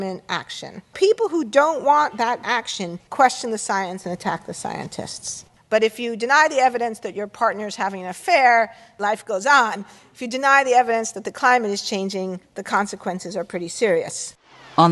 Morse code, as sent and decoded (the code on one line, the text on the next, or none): none